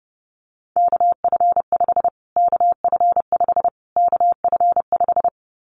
Morse code, heard as a keyed tone, about 30 wpm, 700 Hz. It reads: KF5 KF5 KF5
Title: Callsign Prefixes for Top 25 DXCC - Repeated 3x - 30wpm